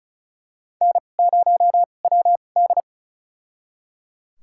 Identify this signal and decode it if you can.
Morse code, heard as a keyed tone, about 36 wpm, spaced as for 25 wpm, 700 Hz. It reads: N0WD
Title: USA Call Signs - 4 Characters - 2x Character Spacing - 35wpm